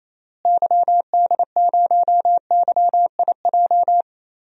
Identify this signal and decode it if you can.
Morse code, heard as a keyed tone, about 28 wpm, 700 Hz. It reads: YD0YIJ
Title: DXCC-100 Call Signs - 6 Characters - Repeated 3x - 28wpm